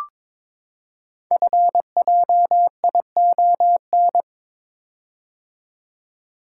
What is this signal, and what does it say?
Morse code, heard as a keyed tone, about 23 wpm, 700 Hz. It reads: FJION
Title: Code Groups - 5 Letters - 22wpm